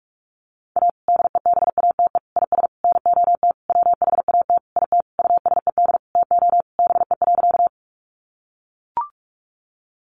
Morse code, heard as a keyed tone, about 44 wpm, 700 Hz. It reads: A DEBATE IS NOT WHAT IT USED TO BE.
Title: Sentences from Top 1100 Words 45wpm